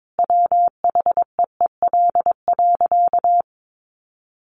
Morse code, heard as a keyed tone, about 22 wpm, 700 Hz. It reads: WHEEL.
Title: Additional Easy Phrases 22wpm